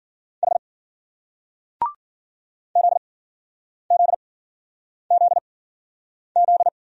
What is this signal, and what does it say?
Morse code, heard as a keyed tone, about 54 wpm, spaced as for 9 wpm, 700 Hz. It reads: H ZZZZ